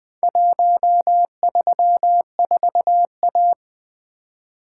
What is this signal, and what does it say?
Morse code, heard as a keyed tone, about 20 wpm, 700 Hz. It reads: 134A
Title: Four Letter-Number Combinations - 20wpm